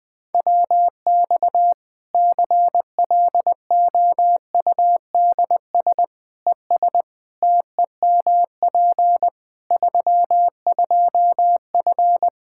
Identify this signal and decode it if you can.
Morse code, heard as a keyed tone, about 20 wpm, 700 Hz. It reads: WX CLOUDS ES TEMP 32F
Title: Practice QSO - Exchange 2 - Info - 20wpm